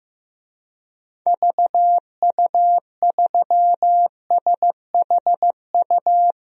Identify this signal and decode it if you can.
Morse code, heard as a keyed tone, about 15 wpm, 700 Hz. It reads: VU3SHU
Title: DXCC-100 Call Signs - 6 Characters - Repeated 2x - 15wpm